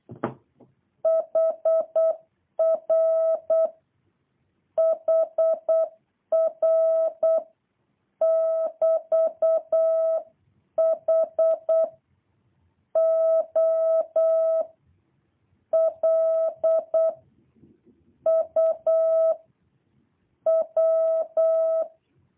Morse code, heard as a keyed tone, about 8 wpm, 650 Hz. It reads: HR HR <BT> H O L U W
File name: jh8cbh / 2023cw